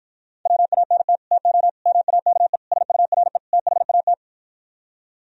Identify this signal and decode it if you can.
Morse code, heard as a keyed tone, about 55 wpm, 700 Hz. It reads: WANT TO MAKE SURE THAT